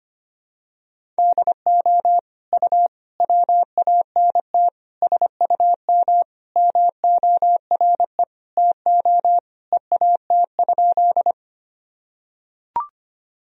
Morse code, heard as a keyed tone, about 25 wpm, 700 Hz. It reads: DO U WANT SUM MORE TO EAT?
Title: Sentences from Top 300 Words and Abbreviations 25wpm